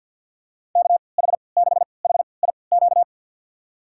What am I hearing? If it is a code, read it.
Morse code, heard as a keyed tone, about 50 wpm, spaced as for 28 wpm, 700 Hz. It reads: KH6HIQ
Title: USA Call Signs - 6 Characters - 3x Character Spacing - 50wpm